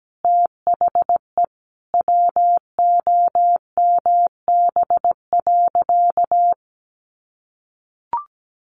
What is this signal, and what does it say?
Morse code, heard as a keyed tone, about 17 wpm, 700 Hz. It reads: THE WOMB.
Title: A Sound Of Thunder 17wpm